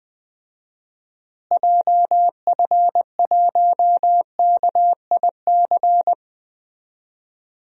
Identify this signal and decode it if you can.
Morse code, heard as a keyed tone, about 21 wpm, 700 Hz. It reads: JF1KIC
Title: DXCC-100 Call Signs - 6 Characters - Repeated 3x - 20wpm